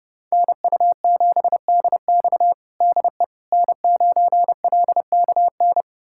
Morse code, heard as a keyed tone, about 30 wpm, 700 Hz. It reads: NU7DX DE N9LKN
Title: USA Callsign DE Callsign - 5 Characters - 30wpm